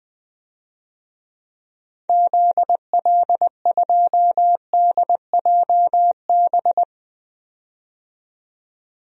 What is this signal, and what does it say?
Morse code, heard as a keyed tone, about 20 wpm, 700 Hz. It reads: ZL2DJB
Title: DXCC-100 Call Signs - 6 Characters - 20wpm